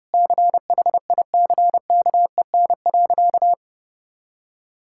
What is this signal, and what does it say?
Morse code, heard as a keyed tone, about 32 wpm, 700 Hz. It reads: CHICKEN.
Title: Sentences from Top 1000 Words - Review Part 5 - 30wpm